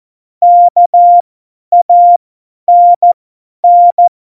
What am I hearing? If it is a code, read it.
Morse code, heard as a keyed tone, about 14 wpm, spaced as for 10 wpm, 700 Hz. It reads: KANN
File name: Morsekurs / Witze / Witz 06. November 2018, 50 F70